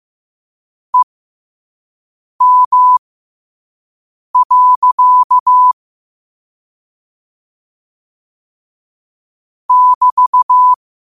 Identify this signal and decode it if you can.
Morse code, heard as a keyed tone, about 15 wpm, spaced as for 5 wpm, 1000 Hz. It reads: EM. <BT>
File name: qotdfiles / 5wpm